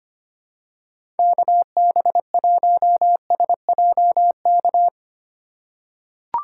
Morse code, heard as a keyed tone, about 25 wpm, 700 Hz. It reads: KB1SJK
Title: USA Call Signs - 6 Characters - 3x Character Spacing - 25wpm